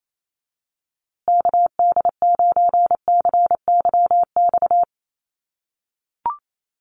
Morse code, heard as a keyed tone, about 28 wpm, 700 Hz. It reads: KD9CYX
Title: USA Call Signs Repeated 3x 28wpm